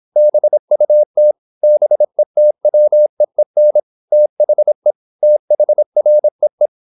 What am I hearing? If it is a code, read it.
Morse code, heard as a keyed tone, about 26 wpm, 600 Hz. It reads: BUT BETWEEN THE THREE
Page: Never 26wpm | CW med Gnister